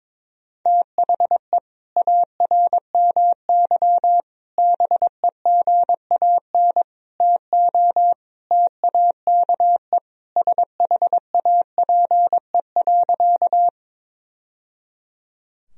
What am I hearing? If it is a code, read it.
Morse code, heard as a keyed tone, about 22 wpm, 700 Hz. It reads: THE ARMY BEGAN TO TAKE SHAPE.
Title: Sentences from Top 1000 Words - Review Part 3 - 22wpm